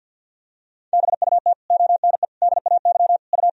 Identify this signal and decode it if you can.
Morse code, heard as a keyed tone, about 50 wpm, 700 Hz. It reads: BUT ONE DAY U
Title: Sentences from Top 400 Words and Abbreviations 50wpm